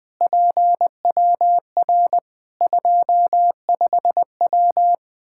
Morse code, heard as a keyed tone, about 20 wpm, 700 Hz. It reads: PWR 25W
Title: Practice QSO - Exchange 2 - Info - 20wpm